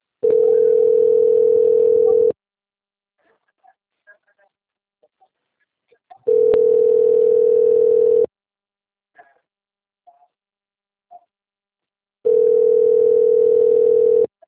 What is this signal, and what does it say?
Ringback tone (North America), 440 + 480 Hz.
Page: Itu - SP